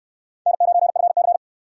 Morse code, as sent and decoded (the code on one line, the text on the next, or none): -. ----. ..-. -.-.
N9FC